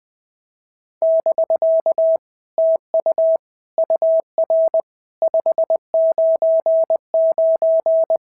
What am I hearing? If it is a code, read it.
Morse code, heard as a keyed tone, about 20 wpm, 650 Hz. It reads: <BK> TU UR 599
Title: POTA QSO Practice - 20wpm